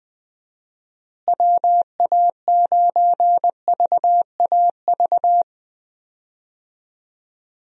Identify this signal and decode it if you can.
Morse code, heard as a keyed tone, about 20 wpm, 700 Hz. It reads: WA9VAV
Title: USA Call Signs - 6 Characters - Repeated 3x - 20wpm